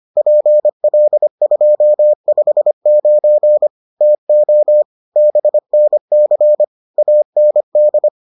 Morse code, heard as a keyed tone, about 25 wpm, 600 Hz. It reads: PL259 TO BNC AND